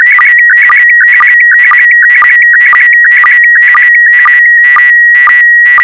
SSTV (slow-scan television), PD 120.